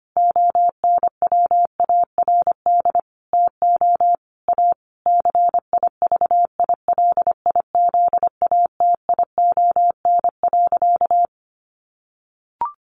Morse code, heard as a keyed tone, about 25 wpm, 700 Hz. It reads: ONWARD TO A CIVILIZATION.
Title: A Sound Of Thunder 25wpm